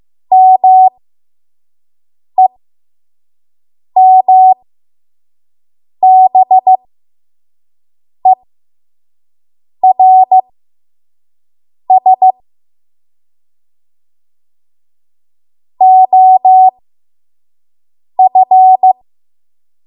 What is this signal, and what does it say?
Morse code, heard as a keyed tone, about 15 wpm, spaced as for 5 wpm, 750 Hz. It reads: MEMBERS OF
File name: Morse / 05WPM